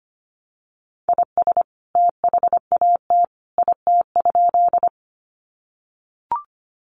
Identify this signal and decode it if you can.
Morse code, heard as a keyed tone, about 25 wpm, 700 Hz. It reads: IS THAT IT?
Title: Sentences from Top 100 Words 25wpm